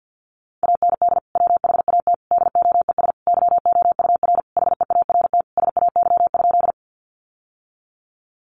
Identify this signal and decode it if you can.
Morse code, heard as a keyed tone, about 50 wpm, 700 Hz. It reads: AND WHAT DOES YOUR HEART SAY?
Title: Sentences from Top 1200 Words 50wpm